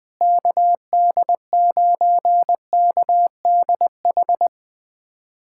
Morse code, heard as a keyed tone, about 20 wpm, 700 Hz. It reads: KD9KDH
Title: USA Callsign DE Callsign - 6 Characters - 20wpm